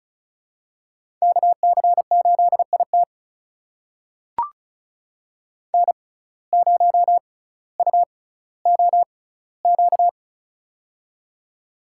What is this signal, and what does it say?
Morse code, heard as a keyed tone, about 34 wpm, spaced as for 18 wpm, 700 Hz. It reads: KC8IT N0UOQ
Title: USA Call Signs - 5 Characters - 6x Character Spacing - 35wpm